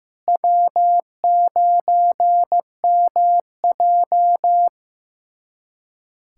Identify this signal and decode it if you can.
Morse code, heard as a keyed tone, about 15 wpm, 700 Hz. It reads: W9MJ
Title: USA Callsign DE Callsign - 4 Characters - 15wpm